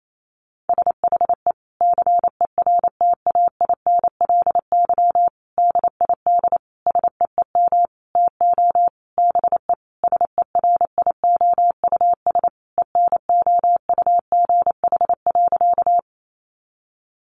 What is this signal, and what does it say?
Morse code, heard as a keyed tone, about 28 wpm, 700 Hz. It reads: SHE CERTAINLY DID SEEM TO BE SERIOUS ENOUGH.
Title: Sentences from Top 1100 Words 28wpm